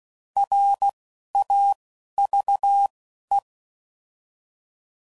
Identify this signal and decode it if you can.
Morse code, heard as a keyed tone, about 16 wpm, 800 Hz. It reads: R A V E
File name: Prefixes / By First Letter / Rs